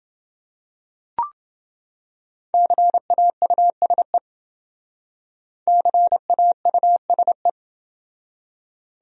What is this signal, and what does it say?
Morse code, heard as a keyed tone, about 28 wpm, 700 Hz. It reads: CAUSE CAUSE